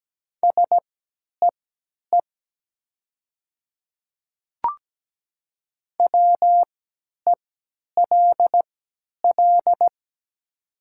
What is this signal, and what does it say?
Morse code, heard as a keyed tone, about 17 wpm, spaced as for 10 wpm, 700 Hz. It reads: SEE WELL
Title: 3x Character Spacing - Top 300 Words - 17wpm